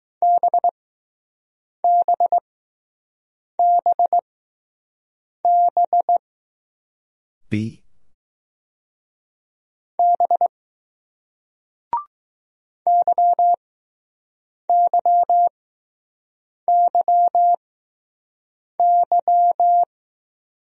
Morse code, heard as a keyed tone, about 20 wpm, spaced as for 6 wpm, 700 Hz. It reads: BBBB B YYYY